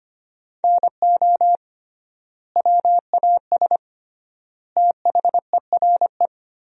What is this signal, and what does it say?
Morse code, heard as a keyed tone, about 25 wpm, 700 Hz. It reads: NO WAS THERE
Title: 3x Word-Spacing - Sentences from Top 100 Words - 25wpm